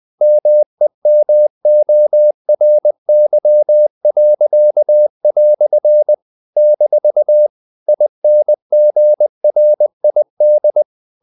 Morse code, heard as a keyed tone, about 20 wpm, 600 Hz. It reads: MEMORY." - INGRID
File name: CW / Practice / EnglishQuotations0320